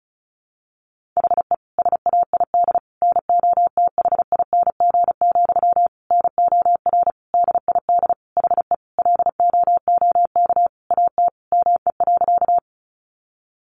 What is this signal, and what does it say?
Morse code, heard as a keyed tone, about 34 wpm, 700 Hz. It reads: HE SAID NOTHING, NOR DID HE LOOK AT ME.
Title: Sentences from Top 1500 Words 35wpm